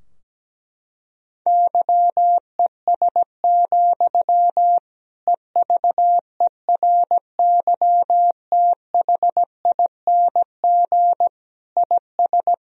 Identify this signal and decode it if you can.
Morse code, heard as a keyed tone, about 17 wpm, 700 Hz. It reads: YES, EVERYTHING IS
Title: Sentences from Top 1000 Words - Review Part 2 - 17wpm